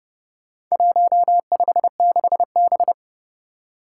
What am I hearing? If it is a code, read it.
Morse code, heard as a keyed tone, about 32 wpm, 700 Hz. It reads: 156B
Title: Four Challenging Letter-Number Combinations 30wpm